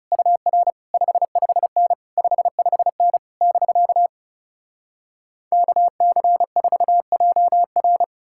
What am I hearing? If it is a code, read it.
Morse code, heard as a keyed tone, about 40 wpm, 700 Hz. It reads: UR 55N 55N <BK> KC4JR